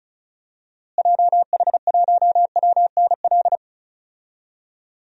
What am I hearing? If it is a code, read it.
Morse code, heard as a keyed tone, about 36 wpm, 700 Hz. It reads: JH1WNL
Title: DXCC-100 Call Signs - 6 Characters - Repeated 3x - 35wpm